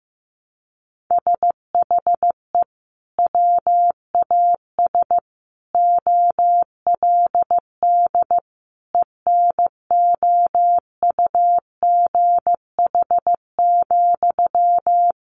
Morse code, heard as a keyed tone, about 15 wpm, 700 Hz. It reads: SHE WAS OLD ENOUGH,
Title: Sentences from Top 800 Words 15wpm